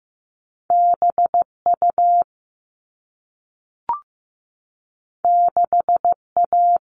Morse code, heard as a keyed tone, about 15 wpm, 700 Hz. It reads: BU 6A
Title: All Cut Numbers - 2-Digit 15wpm